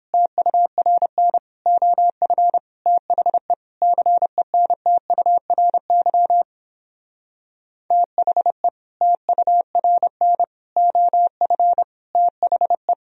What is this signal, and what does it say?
Morse code, heard as a keyed tone, about 29 wpm, 700 Hz. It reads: TURN OF THE CENTURY THE TURN OF THE